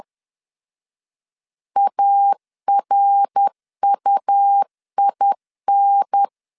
Morse code, heard as a keyed tone, about 10 wpm, 800 Hz. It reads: ARUIN
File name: LEZIONI / 6 PKX0 / 6 PKX 10wpm 2426